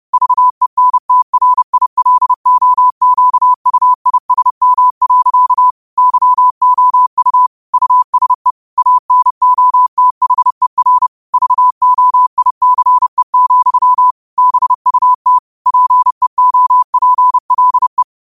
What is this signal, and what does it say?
Morse code, heard as a keyed tone, about 30 wpm, 1000 Hz. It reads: VENTRILOQUISM. YOU USE ANOTHER VOICE, BUT PEOPLE
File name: qotdfiles / 30wpm